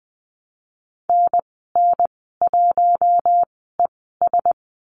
Morse code, heard as a keyed tone, about 20 wpm, spaced as for 14 wpm, 700 Hz. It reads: NN1ES
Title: 042 Lesson 8g - Number 1 - Call Signs 20wpm